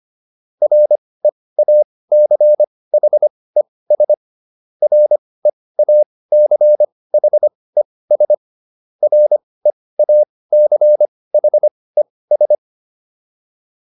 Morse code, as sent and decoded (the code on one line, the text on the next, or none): .-. .- -.-. .... ... / .-. .- -.-. .... ... / .-. .- -.-. .... ...
REACHES REACHES REACHES